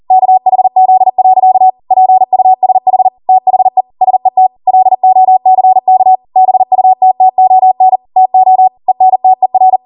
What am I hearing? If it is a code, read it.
Morse code, heard as a keyed tone, about 40 wpm, 750 Hz. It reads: KHZ. PUSH THE SET LOCK BUTTON TO ENTER
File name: Morse / 40WPM